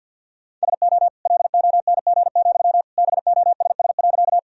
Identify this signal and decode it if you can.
Morse code, heard as a keyed tone, about 50 wpm, 700 Hz. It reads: SO LONG, BOSS.
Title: Sentences from Top 1000 Words - Review Part 4 - 50wpm